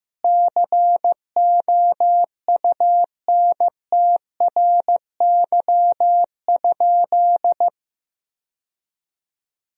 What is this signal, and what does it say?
Morse code, heard as a keyed tone, about 16 wpm, 700 Hz. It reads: COUNTRY?
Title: 3x Word-Spacing - Sentences from Top 300 Words - 15wpm